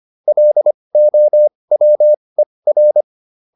Morse code, heard as a keyed tone, about 25 wpm, spaced as for 20 wpm, 600 Hz. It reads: LOWER